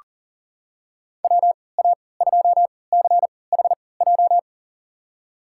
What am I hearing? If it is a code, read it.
Morse code, heard as a keyed tone, about 38 wpm, spaced as for 22 wpm, 700 Hz. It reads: WA2CHJ
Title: USA Call Signs - 6 Characters - 3x Character Spacing - 40wpm